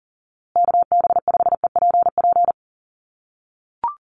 Morse code, heard as a keyed tone, about 40 wpm, 700 Hz. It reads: KB5EPP
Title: USA Call Signs Repeated 2x 40wpm